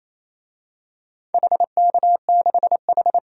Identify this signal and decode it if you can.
Morse code, heard as a keyed tone, about 28 wpm, 700 Hz. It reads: HK6H
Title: DXCC-100 Call Signs - 4 Characters - 28wpm